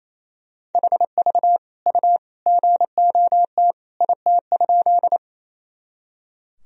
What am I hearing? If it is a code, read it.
Morse code, heard as a keyed tone, about 28 wpm, 700 Hz. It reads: HV U GOT IT?
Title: Sentences from Top 300 Words and Abbreviations 28wpm